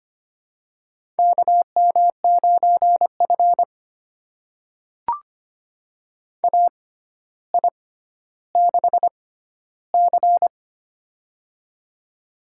Morse code, heard as a keyed tone, about 25 wpm, spaced as for 13 wpm, 700 Hz. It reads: KM9F AI6C
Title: USA Call Signs - 4 Characters - 6x Character Spacing - 25wpm